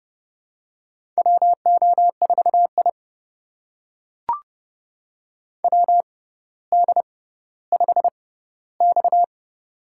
Morse code, heard as a keyed tone, about 31 wpm, spaced as for 16 wpm, 700 Hz. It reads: WO4I WD5X
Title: USA Call Signs - 4 Characters - 6x Character Spacing - 30wpm